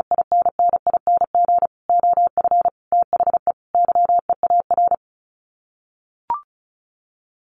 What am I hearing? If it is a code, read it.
Morse code, heard as a keyed tone, about 34 wpm, 700 Hz. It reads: INNING OF THE YEAR
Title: Sets of 5 Words Encore2 35wpm